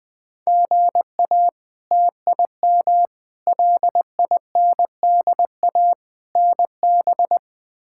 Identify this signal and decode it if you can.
Morse code, heard as a keyed tone, about 20 wpm, 700 Hz. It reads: GA TIM LINDA NB
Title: K1USN SST QSO Practice - 20wpm